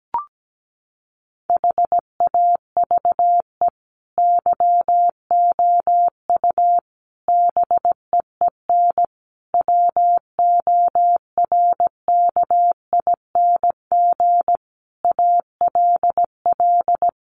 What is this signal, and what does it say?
Morse code, heard as a keyed tone, about 17 wpm, 700 Hz. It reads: HAVE YOU BEEN WORKING ALL
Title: Sentences from Top 700 Words 17wpm